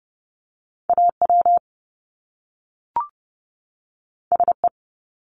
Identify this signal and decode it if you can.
Morse code, heard as a keyed tone, about 30 wpm, 700 Hz. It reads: AW SE
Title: Top Two Letter Combinations in Words 30wpm